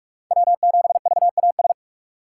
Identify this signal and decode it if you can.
Morse code, heard as a keyed tone, about 46 wpm, 700 Hz. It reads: W7VAS
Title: K1USN SST QSO Practice - 45wpm